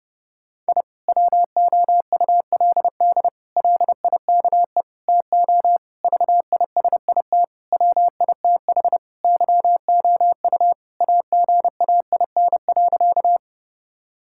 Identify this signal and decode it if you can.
Morse code, heard as a keyed tone, about 30 wpm, 700 Hz. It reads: I WOULD LIKE TO VISIT WITH YOU AGAIN.
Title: Sentences from Top 1000 Words - Review Part 6 - 30wpm